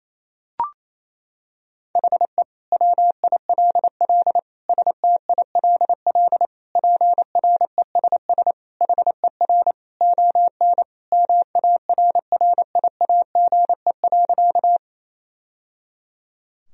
Morse code, heard as a keyed tone, about 28 wpm, 700 Hz. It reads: HE WILL STILL PRESS HER ON MARRIAGE.
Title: Sentences from Top 1000 Words - Review Part 1 - 28wpm